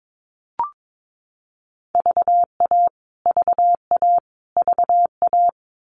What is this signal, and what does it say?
Morse code, heard as a keyed tone, about 22 wpm, 700 Hz. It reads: VA VA VA
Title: US States Repeated 3x 22wpm